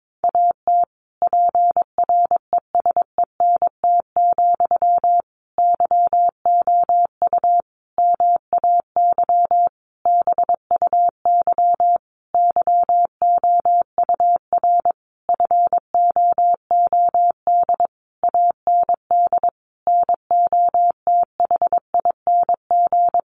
Morse code, heard as a keyed tone, about 22 wpm, 700 Hz. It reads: AT PRESENT, YOU MAY BUY YOUR FOOD AND NOTHING
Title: Sentences from Top 800 Words 22wpm